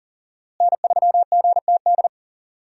Morse code, heard as a keyed tone, about 40 wpm, 700 Hz. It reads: N3GTD
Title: USA Call Signs - 5 Characters - Repeated 2x - 40wpm